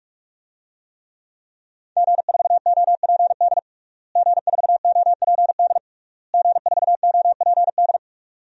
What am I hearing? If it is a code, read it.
Morse code, heard as a keyed tone, about 42 wpm, 700 Hz. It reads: G4OPD G4OPD G4OPD
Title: DXCC-100 Call Signs - 5 Characters - Repeated 3x - 45wpm